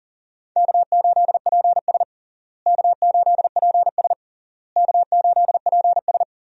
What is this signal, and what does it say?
Morse code, heard as a keyed tone, about 40 wpm, 700 Hz. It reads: K8PS K8PS K8PS
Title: USA Call Signs - 4 Characters - Repeated 3x - 40wpm